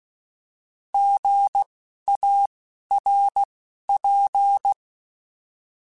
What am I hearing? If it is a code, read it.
Morse code, heard as a keyed tone, about 16 wpm, spaced as for 11 wpm, 800 Hz. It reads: GARP
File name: Alphabetical / Alphabetical mp3s / Gs @10